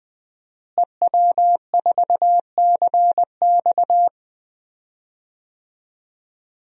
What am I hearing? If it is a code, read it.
Morse code, heard as a keyed tone, about 21 wpm, 700 Hz. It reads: EW4CX
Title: DXCC-100 Call Signs - 5 Characters - Repeated 2x - 20wpm